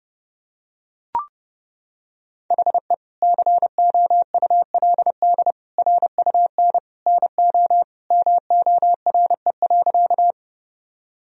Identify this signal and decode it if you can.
Morse code, heard as a keyed tone, about 30 wpm, 700 Hz. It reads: HE COULD RUN NO MORE.
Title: Sentences from Top 1000 Words - Review Part 1 - 30wpm